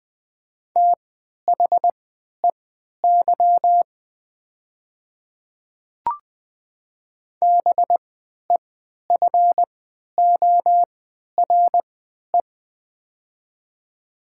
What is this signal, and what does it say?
Morse code, heard as a keyed tone, about 20 wpm, spaced as for 11 wpm, 700 Hz. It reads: THEY BEFORE
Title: 3x Character Spacing - Top 200 Words - 20wpm